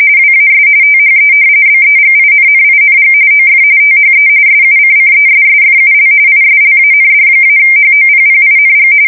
RTTY, about 45.45 baud. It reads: WAS THE BEGINNING OF THE SUCCESSFUL ISLAND-HOPPING
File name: fsk170 / battleshipnj